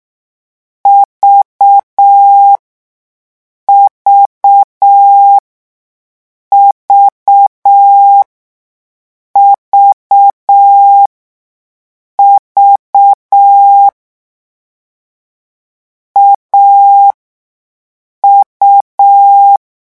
Morse code, heard as a keyed tone, about 6 wpm, spaced as for 5 wpm, 800 Hz. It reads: VVVVV AU
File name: morse mp3 / lecon3-vitesse 6